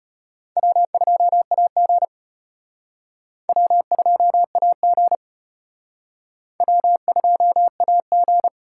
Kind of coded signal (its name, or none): Morse code